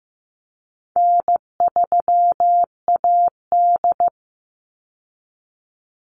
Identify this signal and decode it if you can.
Morse code, heard as a keyed tone, about 15 wpm, 700 Hz. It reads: N3AD
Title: Easy Call Signs Repeated 2x - 15wpm